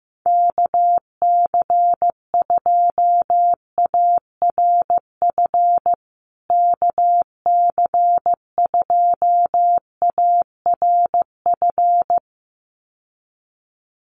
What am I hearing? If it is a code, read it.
Morse code, heard as a keyed tone, about 15 wpm, 700 Hz. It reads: KC2ARF KC2ARF
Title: USA Call Signs Repeated 2x 15wpm